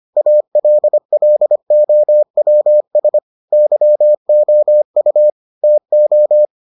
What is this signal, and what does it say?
Morse code, heard as a keyed tone, about 25 wpm, 600 Hz. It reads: ALLOWS YOU TO